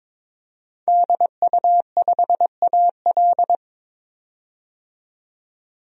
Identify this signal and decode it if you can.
Morse code, heard as a keyed tone, about 23 wpm, 700 Hz. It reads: DU5AL
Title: DXCC-100 Call Signs - 5 Characters - 22wpm